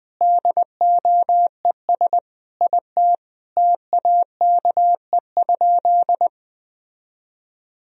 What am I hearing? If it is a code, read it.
Morse code, heard as a keyed tone, about 20 wpm, 700 Hz. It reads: DOES IT TAKE?
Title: Sentences from Top 1000 Words - Review Part 1 - 20wpm